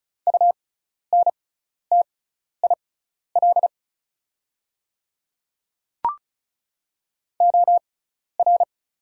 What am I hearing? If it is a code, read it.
Morse code, heard as a keyed tone, about 36 wpm, spaced as for 12 wpm, 700 Hz. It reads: UNTIL OR
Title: 6x Character Spacing - Top 500 Words - 35wpm